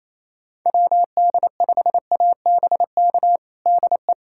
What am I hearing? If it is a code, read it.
Morse code, heard as a keyed tone, about 28 wpm, 700 Hz. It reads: WD5ABK DE
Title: USA Callsign DE Callsign 28wpm